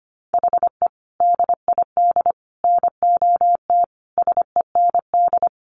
Morse code, heard as a keyed tone, about 25 wpm, 700 Hz. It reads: HE DID NOT SEND
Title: Sentences from Top 1500 Words 25wpm